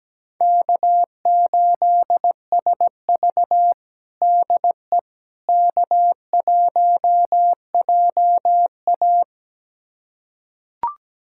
Morse code, heard as a keyed tone, about 17 wpm, 700 Hz. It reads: K8SV DE K1JA
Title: USA Callsign DE Callsign - 4 Characters - 17wpm